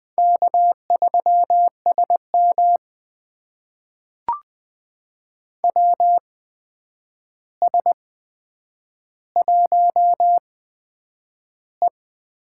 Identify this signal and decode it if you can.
Morse code, heard as a keyed tone, about 20 wpm, spaced as for 9 wpm, 700 Hz. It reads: K3SM WS1E
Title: USA Call Signs - 4 Characters - 8x Character Spacing - 20wpm